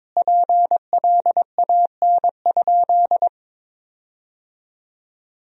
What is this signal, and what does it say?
Morse code, heard as a keyed tone, about 22 wpm, 700 Hz. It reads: PLAN?
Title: 2x Word-Spacing - Sentences from Top 500 Words - 22wpm